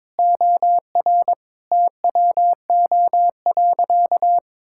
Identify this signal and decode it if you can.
Morse code, heard as a keyed tone, about 22 wpm, 700 Hz. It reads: OR TWO.
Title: Sentences from Top 1000 Words - Review Part 5 - 22wpm